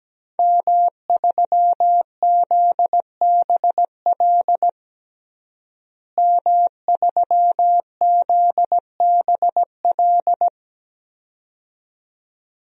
Morse code, heard as a keyed tone, about 17 wpm, 700 Hz. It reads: M3ZBL M3ZBL
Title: DXCC-100 Call Signs - 5 Characters - Repeated 2x - 17wpm